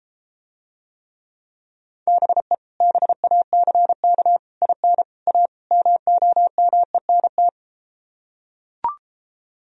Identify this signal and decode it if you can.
Morse code, heard as a keyed tone, about 33 wpm, 700 Hz. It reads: BE BACK IN A MOMENT